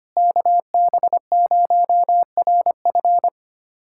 Morse code, heard as a keyed tone, about 25 wpm, 700 Hz. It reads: KB0RF
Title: USA Callsign DE Callsign - 5 Characters - 25wpm